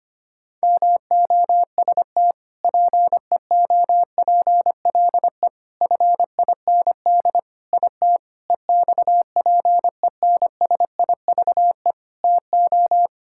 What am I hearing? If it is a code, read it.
Morse code, heard as a keyed tone, about 25 wpm, 700 Hz. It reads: MOST PEOPLE FIND IT EXPENSIVE TO
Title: Sentences from Top 1000 Words - Review Part 6 - 25wpm